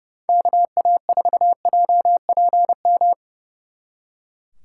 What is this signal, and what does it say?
Morse code, heard as a keyed tone, about 38 wpm, 700 Hz. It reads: KA4JPM